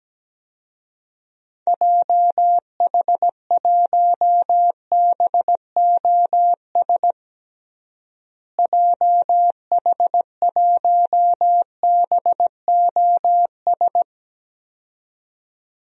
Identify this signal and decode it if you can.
Morse code, heard as a keyed tone, about 17 wpm, 700 Hz. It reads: JH1BOS JH1BOS
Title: DXCC-100 Call Signs - 6 Characters - Repeated 2x - 17wpm